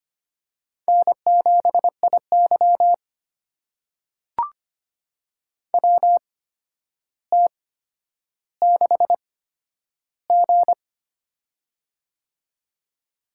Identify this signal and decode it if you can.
Morse code, heard as a keyed tone, about 25 wpm, spaced as for 11 wpm, 700 Hz. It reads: N7IY WT6G
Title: USA Call Signs - 8x Character Spacing - 25wpm